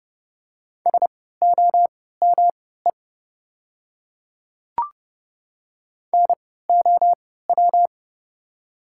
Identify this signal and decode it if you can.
Morse code, heard as a keyed tone, about 30 wpm, spaced as for 17 wpm, 700 Hz. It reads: SOME NOW
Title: 3x Character Spacing - Top 100 Words - 30wpm